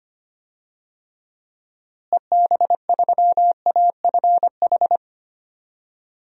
Morse code, heard as a keyed tone, about 25 wpm, 700 Hz. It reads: EB3AFH
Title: DXCC-100 Call Signs - 6 Characters - Repeated 2x - 25wpm